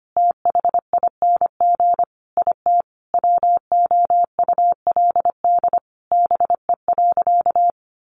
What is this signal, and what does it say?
Morse code, heard as a keyed tone, about 25 wpm, 700 Hz. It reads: THING IT WOULD BE.
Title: Sentences from Top 200 Words 25wpm